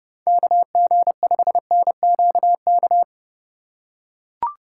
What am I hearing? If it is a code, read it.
Morse code, heard as a keyed tone, about 30 wpm, 700 Hz. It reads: KG5NQK
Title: USA Call Signs - 6 Characters - 3x Character Spacing - 30wpm